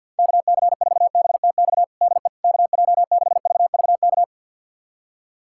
Morse code, heard as a keyed tone, about 47 wpm, 700 Hz. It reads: KC4BTX DE KJ6VVK
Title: USA Callsign DE Callsign - 6 Characters - 50wpm